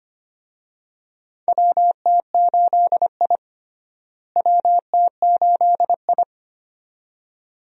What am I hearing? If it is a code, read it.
Morse code, heard as a keyed tone, about 25 wpm, 700 Hz. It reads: WT8I WT8I
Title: USA Call Signs - 4 Characters - Repeated 2x - 25wpm